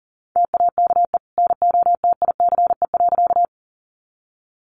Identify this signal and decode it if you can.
Morse code, heard as a keyed tone, about 40 wpm, 700 Hz. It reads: TAKE NOTICE.
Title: Sentences from Top 400 Words 40wpm